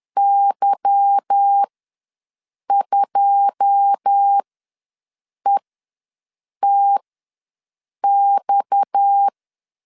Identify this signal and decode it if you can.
Morse code, heard as a keyed tone, about 10 wpm, spaced as for 6 wpm, 800 Hz. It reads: Y2ETX
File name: LEZIONI / 9 2867 ws 10wpm 4215